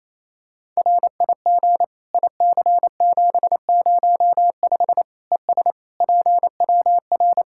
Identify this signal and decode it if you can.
Morse code, heard as a keyed tone, about 27 wpm, 700 Hz. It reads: RIG IC705 ES PWR
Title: Practice QSO - Exchange 2 - Info - 28wpm